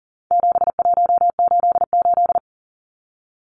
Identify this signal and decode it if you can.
Morse code, heard as a keyed tone, about 40 wpm, 700 Hz. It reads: 7188
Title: Numbers - 4-Digit 40wpm